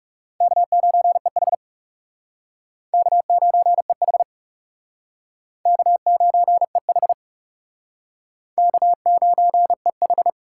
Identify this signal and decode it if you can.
Morse code, heard as a keyed tone, about 35 wpm, 700 Hz. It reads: K9EH K9EH K9EH K9EH
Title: Speed Racing - USA 1x2 Call Signs 30wpm